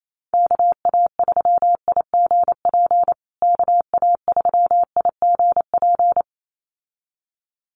Morse code, heard as a keyed tone, about 28 wpm, 700 Hz. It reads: KA3IGP KA3IGP
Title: USA Call Signs Repeated 2x 28wpm